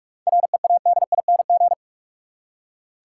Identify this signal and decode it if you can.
Morse code, heard as a keyed tone, about 47 wpm, 700 Hz. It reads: READING
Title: Seven Letter Words - 45wpm